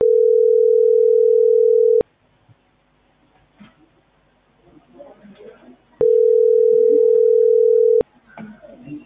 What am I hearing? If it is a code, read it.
Ringback tone (North America), 440 + 480 Hz.